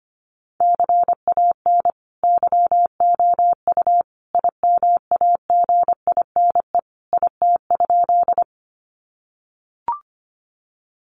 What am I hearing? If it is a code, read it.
Morse code, heard as a keyed tone, about 25 wpm, 700 Hz. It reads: CAN YOU IMAGINE IT?
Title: Sentences from Top 1400 Words 25wpm